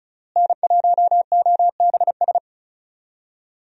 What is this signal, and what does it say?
Morse code, heard as a keyed tone, about 34 wpm, 700 Hz. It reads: N1OBS
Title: USA Call Signs - 5 Characters - Repeated 3x - 35wpm